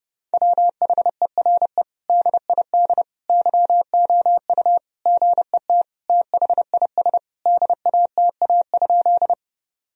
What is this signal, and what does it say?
Morse code, heard as a keyed tone, about 30 wpm, 700 Hz. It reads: WHERE DID YOU GET THIS DATA?
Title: Sentences from Top 1000 Words - Review Part 3 - 30wpm